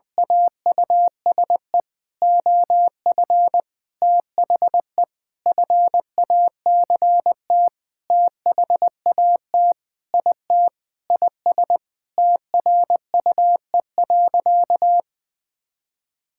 Morse code, heard as a keyed tone, about 20 wpm, 700 Hz. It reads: AUSE OF THE FACT THAT IT IS TRUE.
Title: Sentences with 4-Word Phrases - 20wpm